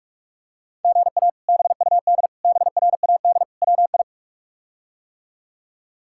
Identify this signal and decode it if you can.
Morse code, heard as a keyed tone, about 46 wpm, 700 Hz. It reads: GA BUD BRAD WI
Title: K1USN SST QSO Practice - 45wpm